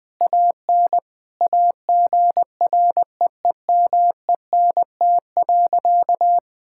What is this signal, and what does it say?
Morse code, heard as a keyed tone, about 19 wpm, 700 Hz. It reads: AN AGREEMENT.
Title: Sentences with 5-Word Phrases - 20wpm